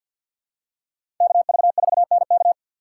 Morse code, heard as a keyed tone, about 52 wpm, 700 Hz. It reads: KV4NK
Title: USA Call Signs - 5 Characters - 2x Character Spacing - 50wpm